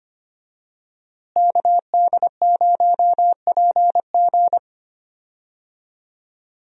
Morse code, heard as a keyed tone, about 25 wpm, 700 Hz. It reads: KD0PG
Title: USA Call Signs - 5 Characters - Repeated 3x - 25wpm